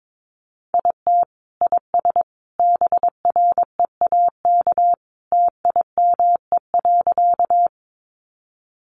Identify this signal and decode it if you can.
Morse code, heard as a keyed tone, about 22 wpm, 700 Hz. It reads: IT IS BREAK TIME.
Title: Easy Phrases 22wpm